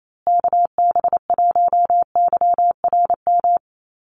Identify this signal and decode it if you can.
Morse code, heard as a keyed tone, about 28 wpm, 700 Hz. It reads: KB1YRM
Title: USA Call Signs Repeated 2x 28wpm